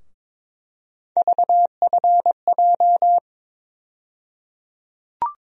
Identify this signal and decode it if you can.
Morse code, heard as a keyed tone, about 22 wpm, 700 Hz. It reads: VFJ
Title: USA Call Sign Postfixes - 3 Letters 22wpm